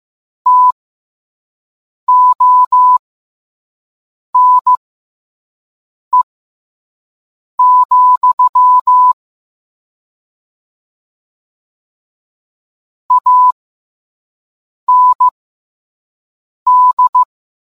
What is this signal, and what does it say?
Morse code, heard as a keyed tone, about 15 wpm, spaced as for 5 wpm, 1000 Hz. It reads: TONE, AND